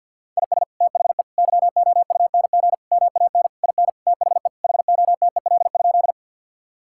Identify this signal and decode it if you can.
Morse code, heard as a keyed tone, about 50 wpm, 700 Hz. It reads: IS THE YOUNG MAN IN THE HOTEL?
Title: Sentences from Top 1600 Words 50wpm